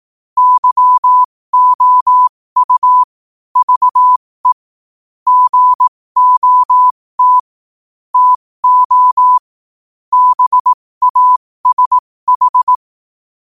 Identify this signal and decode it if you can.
Morse code, heard as a keyed tone, about 18 wpm, spaced as for 15 wpm, 1000 Hz. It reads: YOU VE GOT TO BASH
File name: qotdfiles / 15wpm